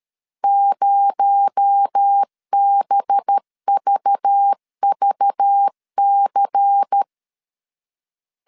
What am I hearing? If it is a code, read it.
Morse code, heard as a keyed tone, about 13 wpm, 800 Hz. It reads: 0BVVC
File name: CW / DET / 1 6 1500C S B 12wpm 3431